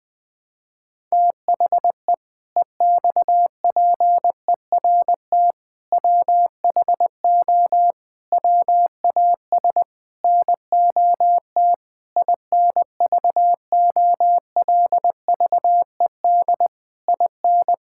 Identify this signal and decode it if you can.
Morse code, heard as a keyed tone, about 20 wpm, 700 Hz. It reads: THE EXPERT WHO WAS NOT INVOLVED IN
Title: Sentences with 7-Word Phrases - 20wpm